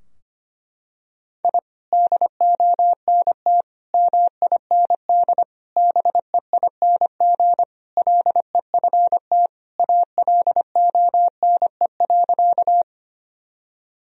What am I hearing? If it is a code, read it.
Morse code, heard as a keyed tone, about 25 wpm, 700 Hz. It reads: I DONT MIND BEING LEFT ALONE.
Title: Sentences from Top 1000 Words - Review Part 3 - 25wpm